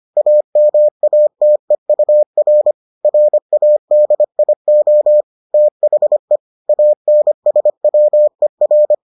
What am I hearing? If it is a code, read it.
Morse code, heard as a keyed tone, about 25 wpm, 600 Hz. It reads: AMATEUR RADIO THE ANSWER